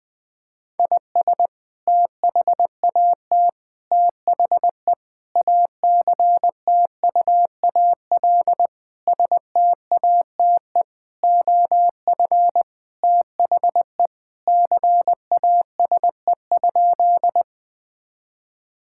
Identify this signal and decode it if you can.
Morse code, heard as a keyed tone, about 20 wpm, 700 Hz. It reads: IS THAT THE ACTUAL STATE OF THE CASE?
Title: Sentences from Top 1000 Words - Review Part 5 - 20wpm